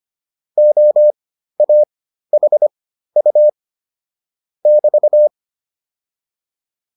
Morse code, heard as a keyed tone, about 25 wpm, spaced as for 13 wpm, 600 Hz. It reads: OAHU <BT>
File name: QSO104 13